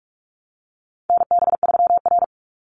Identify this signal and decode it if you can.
Morse code, heard as a keyed tone, about 44 wpm, 700 Hz. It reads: NB3R
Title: Call Signs Repeated 3x - 45wpm